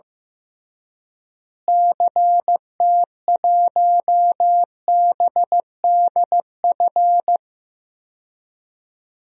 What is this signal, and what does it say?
Morse code, heard as a keyed tone, about 15 wpm, 700 Hz. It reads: CT1BDF
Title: DXCC-100 Call Signs - 6 Characters - Repeated 3x - 15wpm